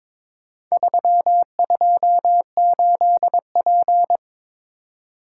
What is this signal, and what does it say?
Morse code, heard as a keyed tone, about 22 wpm, 700 Hz. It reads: 328P
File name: practice / 22wpm / Four Letter-Number Combinations - 22wpm